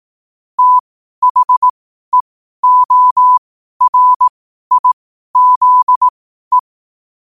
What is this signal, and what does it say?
Morse code, heard as a keyed tone, about 18 wpm, spaced as for 12 wpm, 1000 Hz. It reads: THEORIZE
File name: qotdfiles / 12wpm